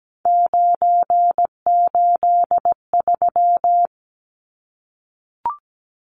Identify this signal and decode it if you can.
Morse code, heard as a keyed tone, about 17 wpm, 700 Hz. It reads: 983
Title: Numbers - 3-Digit 17wpm